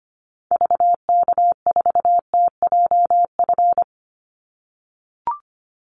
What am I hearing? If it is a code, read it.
Morse code, heard as a keyed tone, about 25 wpm, 700 Hz. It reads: VK4TJF
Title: Call Signs - 25wpm